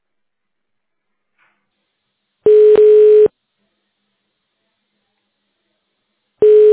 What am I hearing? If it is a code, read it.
Ringback tone (425 Hz, 1 s on, 3 s off).
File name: 1040.3000.55040fe9-03df-4949-b06f- badb4